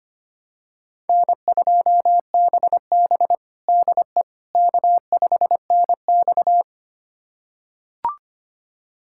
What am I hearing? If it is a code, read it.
Morse code, heard as a keyed tone, about 25 wpm, 700 Hz. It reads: N2BB DE K5NX
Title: USA Callsign DE Callsign - 4 Characters - 25wpm